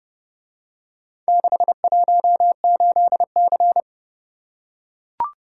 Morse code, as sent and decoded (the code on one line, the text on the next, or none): -.... .---- ---.. -.-.
618C